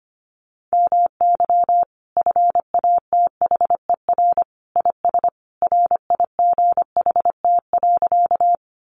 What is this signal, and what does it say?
Morse code, heard as a keyed tone, about 25 wpm, 700 Hz. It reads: MY FATHER IS RIGHT.
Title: Sentences from Top 300 Words 25wpm